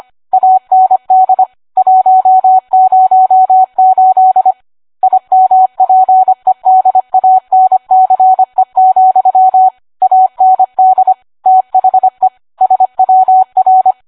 Morse code, heard as a keyed tone, about 25 wpm, 750 Hz, with radio noise. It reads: AND 108 IMPEDANCE, AND THE SWR